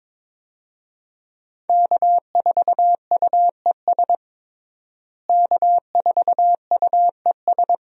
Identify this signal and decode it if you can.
Morse code, heard as a keyed tone, about 22 wpm, 700 Hz. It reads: K4UES K4UES
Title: USA Call Signs - 5 Characters - Repeated 3x - 22wpm